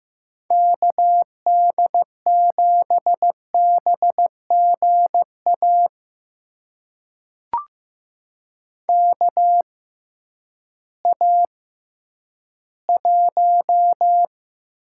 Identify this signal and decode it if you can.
Morse code, heard as a keyed tone, about 15 wpm, spaced as for 10 wpm, 700 Hz. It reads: KD7BGA KA1
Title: USA Call Signs - 6 Characters - 6x Character Spacing - 15wpm